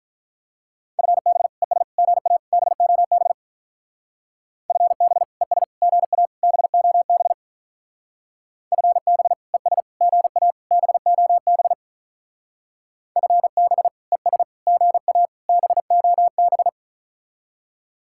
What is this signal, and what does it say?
Morse code, heard as a keyed tone, about 45 wpm, 700 Hz. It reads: FB ES GA BOB FB ES GA BOB FB ES GA BOB FB ES GA BOB